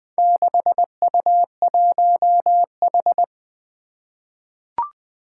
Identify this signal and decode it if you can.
Morse code, heard as a keyed tone, about 20 wpm, 700 Hz. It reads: 6U1H
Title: Four Challenging Letter-Number Combinations 20wpm